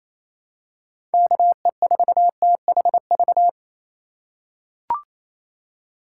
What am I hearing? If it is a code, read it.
Morse code, heard as a keyed tone, about 28 wpm, 700 Hz. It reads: KE4THV
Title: USA Call Signs - 6 Characters - 4x Character Spacing - 28wpm